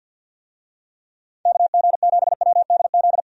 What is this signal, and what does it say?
Morse code, heard as a keyed tone, about 52 wpm, 700 Hz. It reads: KG7WDZ
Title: USA Call Signs - 6 Characters - 50wpm